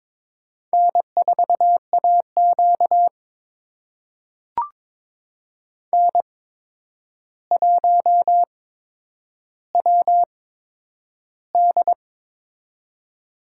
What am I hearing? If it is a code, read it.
Morse code, heard as a keyed tone, about 22 wpm, spaced as for 9 wpm, 700 Hz. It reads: N4AQ N1WD
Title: USA Call Signs - 4 Characters - 8x Character Spacing - 22wpm